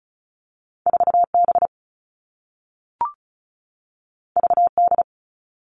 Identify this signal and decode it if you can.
Morse code, heard as a keyed tone, about 36 wpm, 700 Hz. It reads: 4B VD
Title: All Cut Numbers - 2-Digit 35wpm